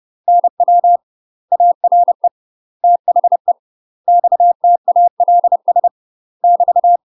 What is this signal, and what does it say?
Morse code, heard as a keyed tone, about 30 wpm, 700 Hz. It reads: NW ARE THE XTALS <BT>
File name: UK MASTER QSOs TRAINING / 30WPM / UK MASTER QSOs 0007